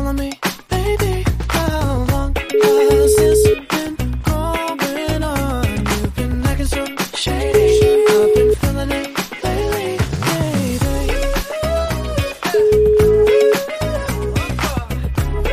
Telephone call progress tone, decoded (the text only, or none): ringback tone